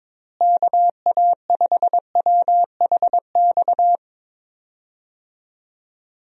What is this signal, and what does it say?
Morse code, heard as a keyed tone, about 23 wpm, 700 Hz. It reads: KA5WHX
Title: USA Call Signs - 6 Characters - Repeated 3x - 22wpm